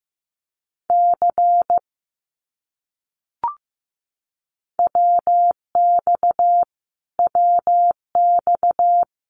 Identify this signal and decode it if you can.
Morse code, heard as a keyed tone, about 15 wpm, 700 Hz. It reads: C WX WX
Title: Abbreviations Repeated 3x - 15wpm